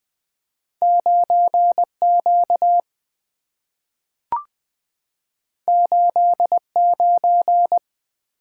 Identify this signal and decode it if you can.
Morse code, heard as a keyed tone, about 20 wpm, 700 Hz. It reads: 9Q 89
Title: Two Letter-Number Combinations - 20wpm